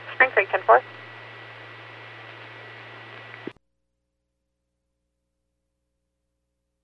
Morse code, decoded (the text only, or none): none